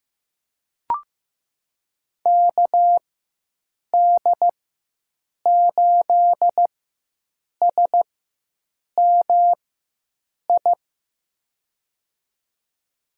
Morse code, heard as a keyed tone, about 16 wpm, spaced as for 7 wpm, 700 Hz. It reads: KD8SMI
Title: USA Call Signs - 6 Characters - 4x Character Spacing - 15wpm